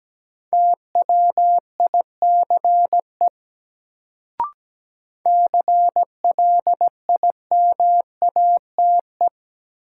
Morse code, heard as a keyed tone, about 17 wpm, 700 Hz. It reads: TWICE CLIMATE
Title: Sentences from Top 1600 Words - New Word Drill - 17wpm